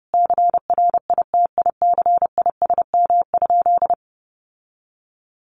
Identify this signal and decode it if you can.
Morse code, heard as a keyed tone, about 30 wpm, 700 Hz. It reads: CRITICISM?
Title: Sentences from Top 1300 Words 30wpm